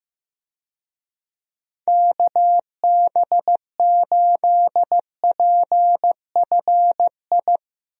Morse code, heard as a keyed tone, about 15 wpm, 700 Hz. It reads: KB8PFI
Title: USA Call Signs - 6 Characters - Repeated 2x - 15wpm